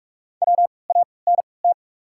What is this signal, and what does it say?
Morse code, heard as a keyed tone, about 44 wpm, spaced as for 25 wpm, 690 Hz. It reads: WANT
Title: 3x Character Spacing - Top 500 Words - 45wpm